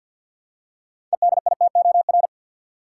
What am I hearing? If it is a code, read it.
Morse code, heard as a keyed tone, about 52 wpm, 700 Hz. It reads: EDITOR
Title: Six Letter Words - 50wpm